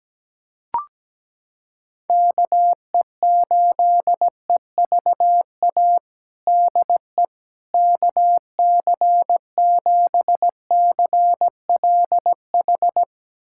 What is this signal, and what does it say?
Morse code, heard as a keyed tone, about 17 wpm, 700 Hz. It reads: KE8EVA DE KC7CLH
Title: USA Callsign DE Callsign 17wpm